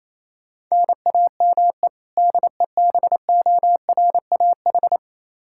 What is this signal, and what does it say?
Morse code, heard as a keyed tone, about 28 wpm, 700 Hz. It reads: NAME DEBORAH
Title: Practice QSO - Exchange 1 - Beginning - 28wpm